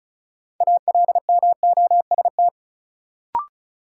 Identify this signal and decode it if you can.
Morse code, heard as a keyed tone, about 36 wpm, 700 Hz. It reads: ALMOST
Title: Sentences from Top 700 Words - New Word Drill - 35wpm